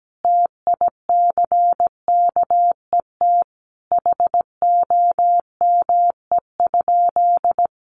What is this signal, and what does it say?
Morse code, heard as a keyed tone, about 17 wpm, 700 Hz. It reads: TICKET HOME?
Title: Sentences from Top 1400 Words 17wpm